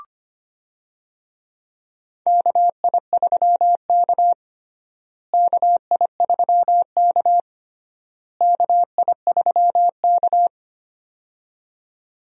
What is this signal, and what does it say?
Morse code, heard as a keyed tone, about 25 wpm, 700 Hz. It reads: KI3K KI3K KI3K
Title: USA Call Signs - 4 Characters - Repeated 3x - 25wpm